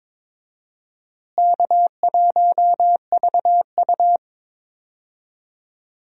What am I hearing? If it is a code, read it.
Morse code, heard as a keyed tone, about 22 wpm, 700 Hz. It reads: K1VU
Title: USA Call Signs - 4 Characters - Repeated 3x - 22wpm